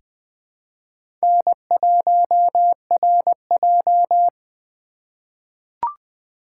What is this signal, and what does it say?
Morse code, heard as a keyed tone, about 20 wpm, 700 Hz. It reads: N1RJ
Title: USA Call Signs - 4 Characters - 2x Character Spacing - 20wpm